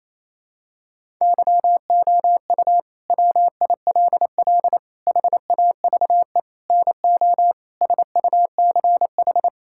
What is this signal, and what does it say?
Morse code, heard as a keyed tone, about 27 wpm, 700 Hz. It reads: YOU WILL HAVE NO SUCH
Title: Sentences from Top 1000 Words - Review Part 6 - 28wpm